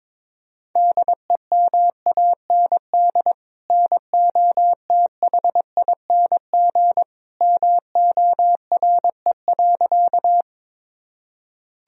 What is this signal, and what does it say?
Morse code, heard as a keyed tone, about 22 wpm, 700 Hz. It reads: DEMAND NOTHING MORE.
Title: Sentences from Top 1000 Words - Review Part 4 - 22wpm